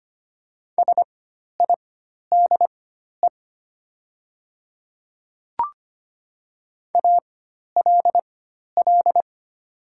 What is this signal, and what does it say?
Morse code, heard as a keyed tone, about 25 wpm, spaced as for 12 wpm, 700 Hz. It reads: SIDE ALL
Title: 4x Character Spacing - Top 100 Words - 25wpm